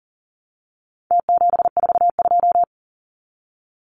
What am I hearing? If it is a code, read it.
Morse code, heard as a keyed tone, about 40 wpm, 700 Hz. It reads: T742
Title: Numbers - Contest Style - 4-Digit 40wpm